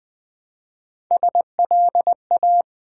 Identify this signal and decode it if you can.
Morse code, heard as a keyed tone, about 20 wpm, 700 Hz. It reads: SLA
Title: Build A Word - 1 Letter At A Time - 3 to 5-Letter Words - 20wpm